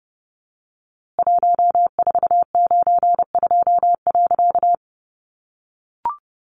Morse code, heard as a keyed tone, about 29 wpm, 700 Hz. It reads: 1492.
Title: A Sound Of Thunder 30wpm